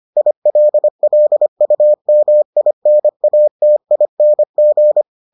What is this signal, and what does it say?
Morse code, heard as a keyed tone, about 25 wpm, 600 Hz. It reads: ILLUMINATING